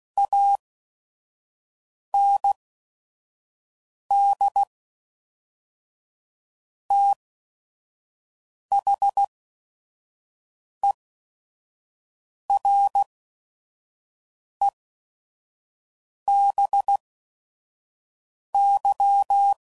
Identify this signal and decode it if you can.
Morse code, heard as a keyed tone, about 16 wpm, spaced as for 5 wpm, 800 Hz. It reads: AND THEREBY
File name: CW Code Groups @5WPM / Technical Subjects / XC